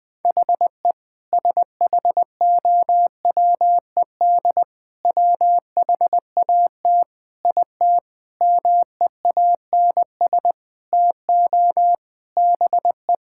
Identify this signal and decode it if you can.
Morse code, heard as a keyed tone, about 20 wpm, 700 Hz. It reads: HE SHOWED WHAT IT MEANS TO BE
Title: Sentences with 5-Word Phrases - 20wpm